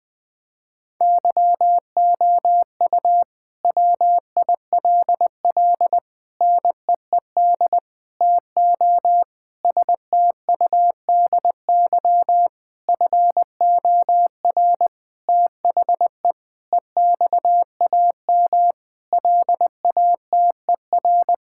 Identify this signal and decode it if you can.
Morse code, heard as a keyed tone, about 20 wpm, 700 Hz. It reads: YOU WILL NEED TO STUDY FOR THE EXAM LATER
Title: Sentences from Top 1000 Words - Review Part 5 - 20wpm